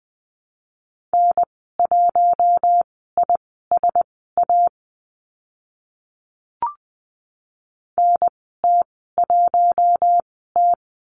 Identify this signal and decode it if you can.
Morse code, heard as a keyed tone, about 20 wpm, spaced as for 15 wpm, 700 Hz. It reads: N1ISA NT1T
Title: 042 Lesson 8g - Number 1 - Call Signs 20wpm